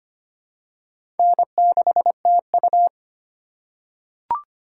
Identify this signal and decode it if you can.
Morse code, heard as a keyed tone, about 25 wpm, 690 Hz. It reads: N6TU
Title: USA Call Signs - 4 Characters - 4x Character Spacing - 25wpm